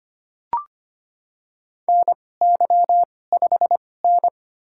Morse code, heard as a keyed tone, about 25 wpm, spaced as for 18 wpm, 700 Hz. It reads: NY5N
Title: USA Call Signs - 4 Characters - 2x Character Spacing - 25wpm